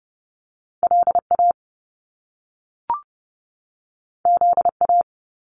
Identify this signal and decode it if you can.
Morse code, heard as a keyed tone, about 30 wpm, 700 Hz. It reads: LA ZA
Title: Top Two Letter Combinations in Words 30wpm